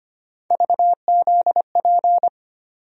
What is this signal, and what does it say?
Morse code, heard as a keyed tone, about 25 wpm, 700 Hz. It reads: VZP
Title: USA Call Sign Postfixes - 3 Letters 25wpm